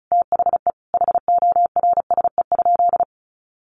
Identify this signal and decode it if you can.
Morse code, heard as a keyed tone, about 34 wpm, 700 Hz. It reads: THE HORSE?
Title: Sentences from Top 500 Words 35wpm